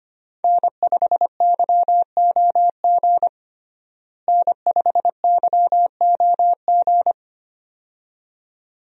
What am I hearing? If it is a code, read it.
Morse code, heard as a keyed tone, about 25 wpm, 700 Hz. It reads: N5YOG N5YOG
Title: USA Call Signs - 5 Characters - Repeated 2x - 25wpm